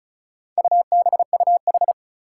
Morse code, heard as a keyed tone, about 36 wpm, 700 Hz. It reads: UBUH
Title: Four Challenging Letter-Number Combinations 35wpm